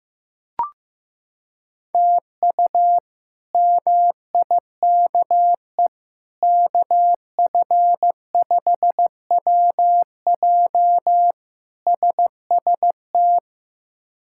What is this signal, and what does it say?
Morse code, heard as a keyed tone, about 15 wpm, 700 Hz. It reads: TU MIKE KF5WJ SST
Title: K1USN SST QSO Practice - 15wpm